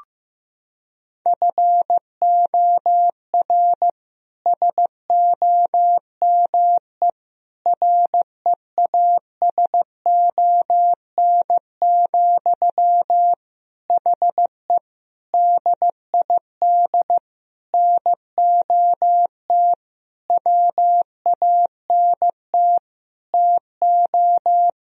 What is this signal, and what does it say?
Morse code, heard as a keyed tone, about 15 wpm, 700 Hz. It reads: FOR SOME REASON, HE DID NOT WANT TO
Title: Sentences from Top 1000 Words - Review Part 3 - 15wpm